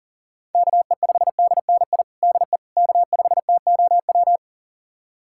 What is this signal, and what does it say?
Morse code, heard as a keyed tone, about 40 wpm, 700 Hz. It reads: KE5DNI DE K5TOW
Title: USA Callsign DE Callsign 40wpm